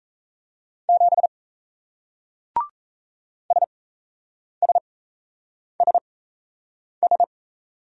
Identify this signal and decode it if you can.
Morse code, heard as a keyed tone, about 38 wpm, spaced as for 8 wpm, 700 Hz. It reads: 7 SSSS